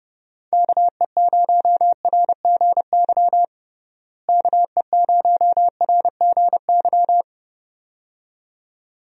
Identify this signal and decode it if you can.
Morse code, heard as a keyed tone, about 32 wpm, 700 Hz. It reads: KE0RGY KE0RGY
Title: USA Call Signs - 6 Characters - Repeated 2x - 30wpm